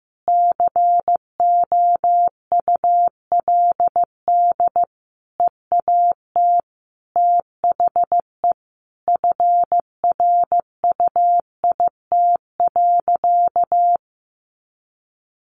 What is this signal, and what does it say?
Morse code, heard as a keyed tone, about 15 wpm, 700 Hz. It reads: COULD EAT THE FRUIT.
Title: Sentences from Top 1300 Words 15wpm